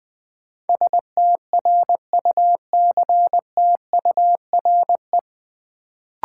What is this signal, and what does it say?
Morse code, heard as a keyed tone, about 21 wpm, 700 Hz. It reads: STRUCTURE
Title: Sentences from Top 1000 Words - Review Part 6 - New Word Drill - 20wpm